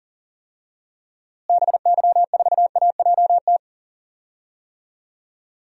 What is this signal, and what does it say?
Morse code, heard as a keyed tone, about 40 wpm, 700 Hz. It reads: BY4AJT
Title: DXCC-100 Call Signs - 6 Characters - Repeated 2x - 40wpm